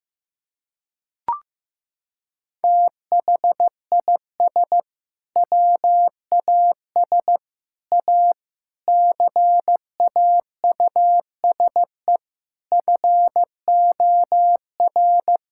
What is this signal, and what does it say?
Morse code, heard as a keyed tone, about 15 wpm, 700 Hz. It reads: THIS WAS A CAUSE FOR
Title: Sentences from Top 1600 Words 15wpm